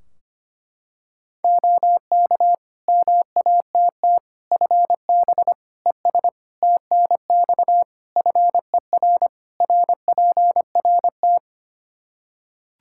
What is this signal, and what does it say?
Morse code, heard as a keyed tone, about 25 wpm, 700 Hz. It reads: OK MATT FB ES TNX FER RPRT
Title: Practice QSO - Exchange 2 - Info - 25wpm